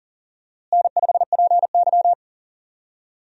Morse code, heard as a keyed tone, about 40 wpm, 700 Hz. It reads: N5PY
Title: USA Call Signs - 4 Characters - Repeated 3x - 40wpm